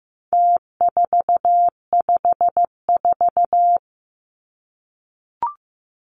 Morse code, heard as a keyed tone, about 15 wpm, 700 Hz. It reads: T454
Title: Numbers - Contest Style - 4-Digit 15wpm